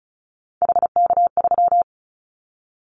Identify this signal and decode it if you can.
Morse code, heard as a keyed tone, about 36 wpm, 700 Hz. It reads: HK3
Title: Callsign Prefixes for Top 100 DXCC - 35wpm